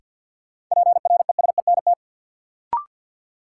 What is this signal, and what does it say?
Morse code, heard as a keyed tone, about 53 wpm, 700 Hz. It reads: PRESENT
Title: Sentences from Top 800 Words - New Word Drill - 50wpm